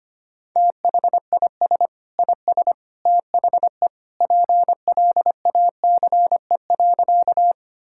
Morse code, heard as a keyed tone, about 25 wpm, 700 Hz. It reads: THIS IS THE PLACE.
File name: Sentences with 3-Word Phrases - 25wpm